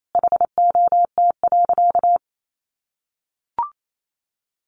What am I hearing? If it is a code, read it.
Morse code, heard as a keyed tone, about 28 wpm, 700 Hz. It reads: HOT.
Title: Sentences from Top 500 Words 28wpm